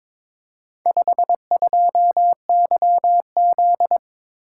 Morse code, heard as a keyed tone, about 23 wpm, 700 Hz. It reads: 52YZ
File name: practice / 22wpm / Four Letter-Number Combinations - 22wpm